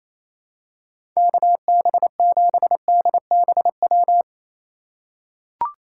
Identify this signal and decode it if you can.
Morse code, heard as a keyed tone, about 28 wpm, 700 Hz. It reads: KB7DBW
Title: USA Call Signs - 6 Characters - 3x Character Spacing - 28wpm